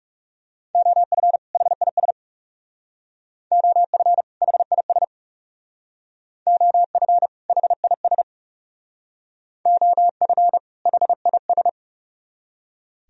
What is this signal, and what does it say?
Morse code, heard as a keyed tone, about 38 wpm, 700 Hz. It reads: OF HIS OF HIS OF HIS OF HIS